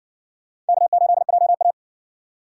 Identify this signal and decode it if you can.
Morse code, heard as a keyed tone, about 60 wpm, 700 Hz. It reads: K8JA